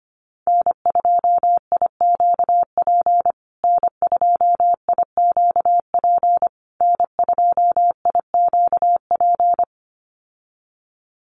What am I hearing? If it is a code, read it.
Morse code, heard as a keyed tone, about 25 wpm, 700 Hz. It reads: N2IQP N2IQP N2IQP
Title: USA Call Signs Repeated 3x 25wpm